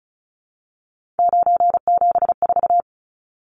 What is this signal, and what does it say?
Morse code, heard as a keyed tone, about 34 wpm, 700 Hz. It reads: 974
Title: Numbers - 3-Digit 35wpm